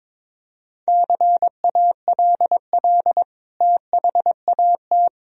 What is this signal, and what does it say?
Morse code, heard as a keyed tone, about 21 wpm, 700 Hz. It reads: CALL THAT
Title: Sentences from Top 1000 Words - Review Part 4 - 22wpm